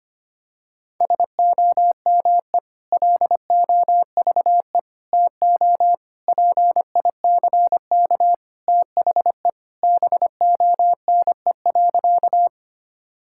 Morse code, heard as a keyed tone, about 25 wpm, 700 Hz. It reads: SOME LOVE TO PICK THE BONE.
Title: Sentences from Top 1600 Words 25wpm